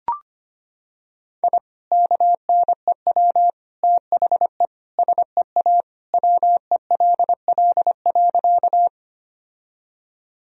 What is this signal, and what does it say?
Morse code, heard as a keyed tone, about 25 wpm, 700 Hz. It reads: I KNEW THE SEA WELL.
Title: Sentences from Top 1000 Words - Review Part 1 - 25wpm